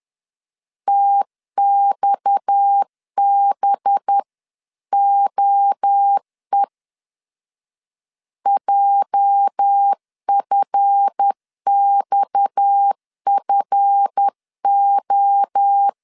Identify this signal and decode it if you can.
Morse code, heard as a keyed tone, about 10 wpm, 800 Hz. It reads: TXBOE JFXFO
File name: LEZIONI / 8 YJ45 / 8 YJ 10wpm 2651